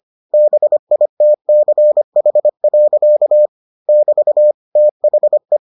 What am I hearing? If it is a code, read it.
Morse code, heard as a keyed tone, about 25 wpm, 600 Hz. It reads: BITCH. <BT> THE